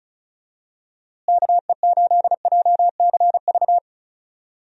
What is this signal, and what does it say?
Morse code, heard as a keyed tone, about 36 wpm, 700 Hz. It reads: KE8JCV
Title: USA Call Signs - 6 Characters - Repeated 2x - 35wpm